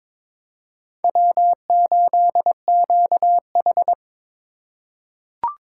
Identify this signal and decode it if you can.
Morse code, heard as a keyed tone, about 22 wpm, 700 Hz. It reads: W8QH
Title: USA Call Signs - 4 Characters - 3x Character Spacing - 22wpm